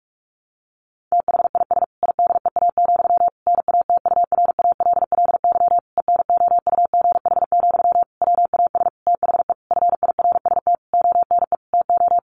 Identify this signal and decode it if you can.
Morse code, heard as a keyed tone, about 44 wpm, 700 Hz. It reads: THIS IDEA, NATURALLY ENOUGH, WAS THE FIRST ONE TO
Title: Sentences from Top 1300 Words 45wpm